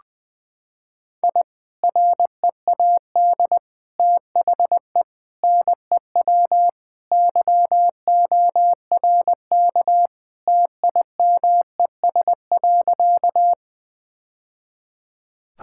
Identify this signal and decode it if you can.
Morse code, heard as a keyed tone, about 20 wpm, 700 Hz. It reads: I READ THE NEW YORK TIMES.
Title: Sentences with 3-Word Phrases - 20wpm